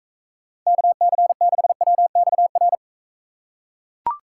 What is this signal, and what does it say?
Morse code, heard as a keyed tone, about 44 wpm, 700 Hz. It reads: KC6WXR